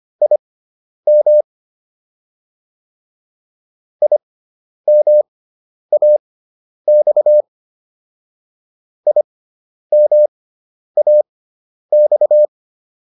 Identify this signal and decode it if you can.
Morse code, heard as a keyed tone, about 25 wpm, spaced as for 10 wpm, 600 Hz. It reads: IM IMAX IMAX